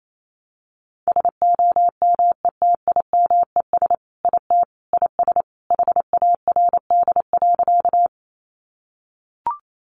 Morse code, heard as a keyed tone, about 28 wpm, 700 Hz. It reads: SOMETIMES IT IS HARD.
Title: Sentences from Top 600 Words 28wpm